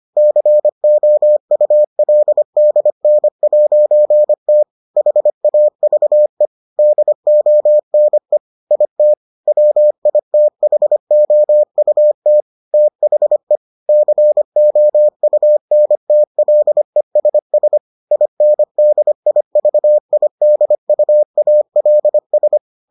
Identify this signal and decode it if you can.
Morse code, heard as a keyed tone, about 25 wpm, 600 Hz. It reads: COULDN'T HAVE DONE IT WITHOUT THE COUNTLESS INDIVIDUALS